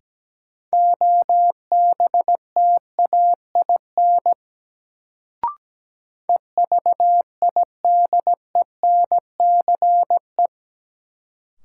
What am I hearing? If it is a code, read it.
Morse code, heard as a keyed tone, about 17 wpm, 700 Hz. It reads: OBTAIN EVIDENCE
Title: Sentences from Top 1300 Words - New Word Drill - 17wpm